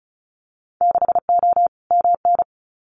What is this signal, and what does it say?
Morse code, heard as a keyed tone, about 34 wpm, 700 Hz. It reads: 6O MN
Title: Winter Field Day Exchanges Encore 35wpm